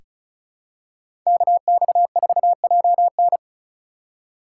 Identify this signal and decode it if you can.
Morse code, heard as a keyed tone, about 34 wpm, 700 Hz. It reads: KX4JN
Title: USA Call Signs - 5 Characters - 4x Character Spacing - 35wpm